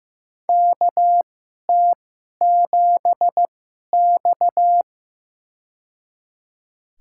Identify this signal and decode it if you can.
Morse code, heard as a keyed tone, about 16 wpm, spaced as for 11 wpm, 700 Hz. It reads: KT7X
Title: USA Call Signs - 4 Characters - 2x Character Spacing - 15wpm